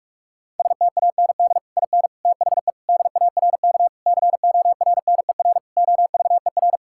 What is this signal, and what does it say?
Morse code, heard as a keyed tone, about 43 wpm, 700 Hz. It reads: STAND IN THE DARK CORNER OVER
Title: Sentences from Top 1600 Words 45wpm